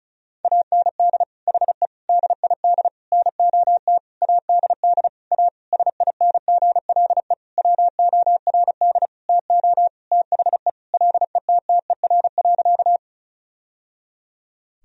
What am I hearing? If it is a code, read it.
Morse code, heard as a keyed tone, about 36 wpm, 700 Hz. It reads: AND HE DID NOT ADD A SINGLE WORD TO THE LETTER.
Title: Sentences from Top 1000 Words - Review Part 1 - 35wpm